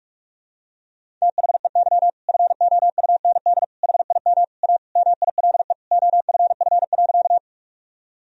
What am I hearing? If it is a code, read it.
Morse code, heard as a keyed tone, about 41 wpm, 700 Hz. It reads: THEY FOUND HIM A MILE OFF.
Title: Sentences from Top 1000 Words - Review Part 1 - 45wpm